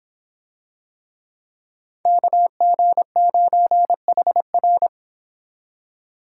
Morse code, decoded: KG9HR

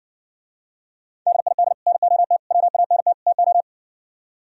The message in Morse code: DID NOT WANT TO

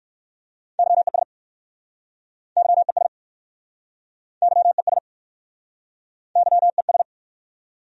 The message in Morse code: YES YES YES YES

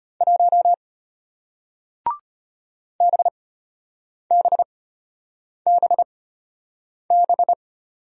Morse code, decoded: 1 BBBB

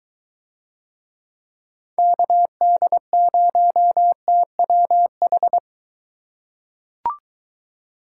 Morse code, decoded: KD0TWH

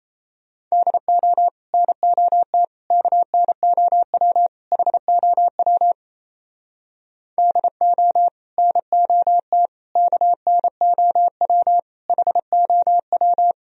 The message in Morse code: DO NOT KNOW HOW DO NOT KNOW HOW

